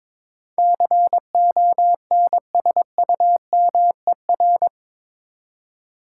Morse code, heard as a keyed tone, about 22 wpm, 700 Hz. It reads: CONSUMER
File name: practice / 22wpm / Eight Letter Words - 22wpm